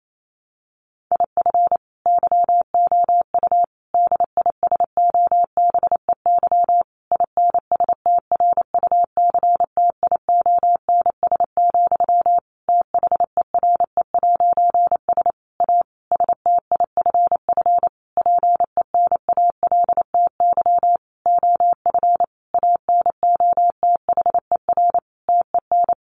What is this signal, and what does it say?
Morse code, heard as a keyed tone, about 28 wpm, 700 Hz. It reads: IF YOU DISOBEY INSTRUCTIONS, THERE'S A STIFF PENALTY OF ANOTHER TEN
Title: A Sound Of Thunder 28wpm